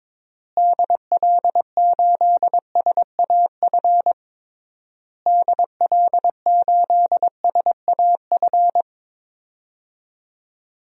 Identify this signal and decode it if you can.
Morse code, heard as a keyed tone, about 21 wpm, 700 Hz. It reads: DL8SAF DL8SAF
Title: DXCC-100 Call Signs - 6 Characters - Repeated 3x - 22wpm